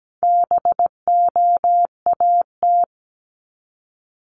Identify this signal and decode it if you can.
Morse code, heard as a keyed tone, about 17 wpm, 700 Hz. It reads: BOAT
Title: Multi-letter Phonemes Repeated 3x 17wpm